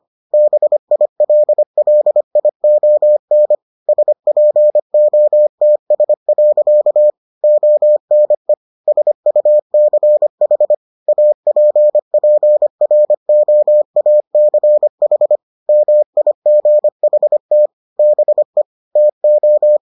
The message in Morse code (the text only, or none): BILLION SPOTS. ONE SUCH APPROACH MIGHT BE TO